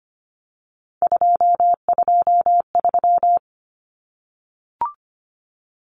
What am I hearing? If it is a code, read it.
Morse code, heard as a keyed tone, about 25 wpm, 700 Hz. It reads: 223
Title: Common Cut Numbers - 3-Digit 25wpm